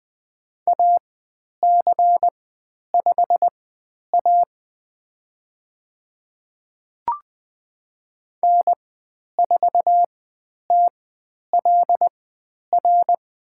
Morse code, heard as a keyed tone, about 20 wpm, spaced as for 10 wpm, 700 Hz. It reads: AC5A N4TLR
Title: 109 Lesson 18g - Letter m - Call Signs 20wpm